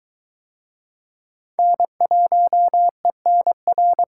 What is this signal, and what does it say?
Morse code, heard as a keyed tone, about 22 wpm, 700 Hz. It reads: N1ENR